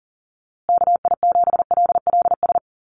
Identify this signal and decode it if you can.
Morse code, heard as a keyed tone, about 40 wpm, 700 Hz. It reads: KI7LLS
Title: USA Call Signs Repeated 3x 40wpm